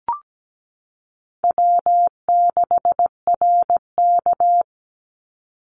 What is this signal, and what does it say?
Morse code, heard as a keyed tone, about 17 wpm, 700 Hz. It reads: W6RK
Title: Easy Call Signs - 17wpm